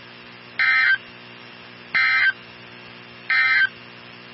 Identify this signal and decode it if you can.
Emergency Alert System (SAME) tones, heard as harsh buzzing bursts. It is heard from 0.5 to 3.8 s.
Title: NOAA Weather Radio 162.550